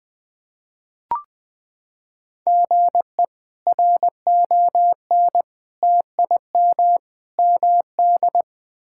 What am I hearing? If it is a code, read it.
Morse code, heard as a keyed tone, about 20 wpm, 700 Hz. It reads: GE RON TIM MD
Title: K1USN SST QSO Practice - 20wpm